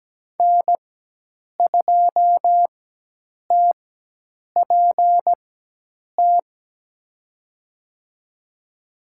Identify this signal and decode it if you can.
Morse code, heard as a keyed tone, about 17 wpm, spaced as for 8 wpm, 700 Hz. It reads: N2TPT
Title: USA Call Signs - 5 Characters - 4x Character Spacing - 17wpm